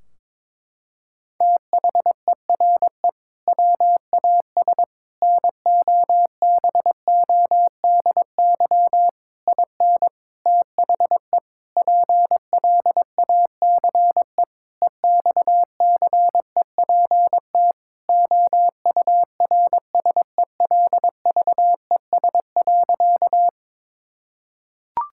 Morse code, heard as a keyed tone, about 22 wpm, 700 Hz. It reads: THERE WAS NOBODY IN THE PLACE EXCEPT OURSELVES.
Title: Sentences from Top 1600 Words 22wpm